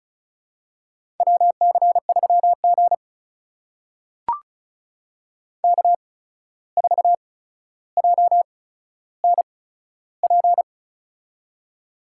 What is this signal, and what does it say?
Morse code, heard as a keyed tone, about 36 wpm, spaced as for 14 wpm, 700 Hz. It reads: WC3G K4JNP
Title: USA Call Signs - 8x Character Spacing - 35wpm